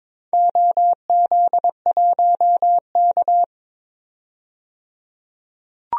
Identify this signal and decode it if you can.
Morse code, heard as a keyed tone, about 21 wpm, 700 Hz. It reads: OZ1K
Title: DXCC-100 Call Signs - 4 Characters - 22wpm